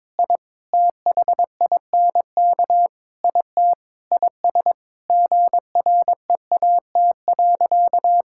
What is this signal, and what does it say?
Morse code, heard as a keyed tone, about 22 wpm, 700 Hz. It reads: I THINK IT IS GREAT.
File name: practice / 22wpm / Sentences with 4-Word Phrases - 22wpm